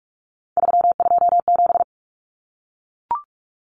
Morse code, heard as a keyed tone, about 44 wpm, 700 Hz. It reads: 327
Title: Common Cut Numbers - 3-Digit 45wpm